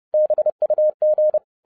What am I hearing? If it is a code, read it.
Morse code, heard as a keyed tone, about 30 wpm, 610 Hz. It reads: BUG
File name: 2011-07-mp3 / n20 GREETINGS FROM SINGAPORE 30wpm